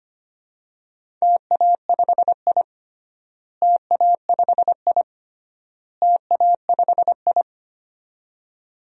Morse code, heard as a keyed tone, about 25 wpm, 700 Hz. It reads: TA5I TA5I TA5I
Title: DXCC-100 Call Signs - 4 Characters - Repeated 3x - 25wpm